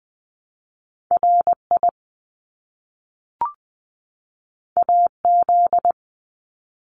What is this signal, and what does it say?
Morse code, heard as a keyed tone, about 20 wpm, 700 Hz. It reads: RI AZ
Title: ARRL Sections 20wpm